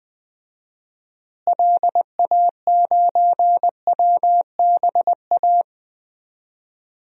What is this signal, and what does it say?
Morse code, heard as a keyed tone, about 20 wpm, 700 Hz. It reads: LA9WBA
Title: DXCC-100 Call Signs - 6 Characters - Repeated 3x - 20wpm